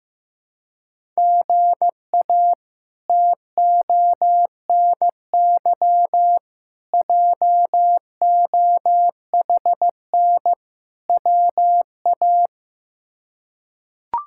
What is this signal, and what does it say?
Morse code, heard as a keyed tone, about 15 wpm, 700 Hz. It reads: GA TONY JOHN WA
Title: K1USN SST QSO Practice - 15wpm